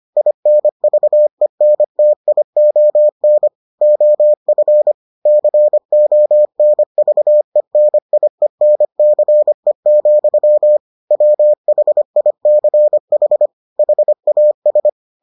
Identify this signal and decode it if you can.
Morse code, heard as a keyed tone, about 25 wpm, 600 Hz. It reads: INVENTION OF CONVENIENCE, WHICH HAS